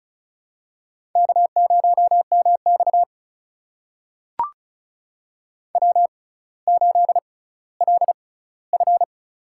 Morse code, heard as a keyed tone, about 36 wpm, spaced as for 18 wpm, 700 Hz. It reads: K0MX W8LF
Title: USA Call Signs - 4 Characters - 6x Character Spacing - 35wpm